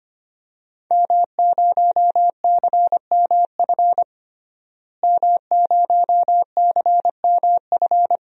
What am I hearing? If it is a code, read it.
Morse code, heard as a keyed tone, about 25 wpm, 700 Hz. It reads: M0CMF M0CMF
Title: DXCC-100 Call Signs - 5 Characters - Repeated 2x - 25wpm